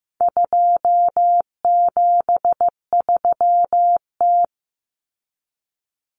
Morse code, heard as a keyed tone, about 15 wpm, 700 Hz. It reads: 273T
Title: Common Cut Numbers - 4-Digit 15wpm